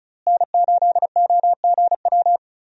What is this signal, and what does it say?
Morse code, heard as a keyed tone, about 36 wpm, 700 Hz. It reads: N8OGW
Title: USA Call Signs - 5 Characters - Repeated 2x - 35wpm